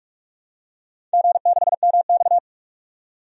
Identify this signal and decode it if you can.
Morse code, heard as a keyed tone, about 46 wpm, 700 Hz. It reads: G6MX
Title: DXCC-100 Call Signs - 4 Characters - 45wpm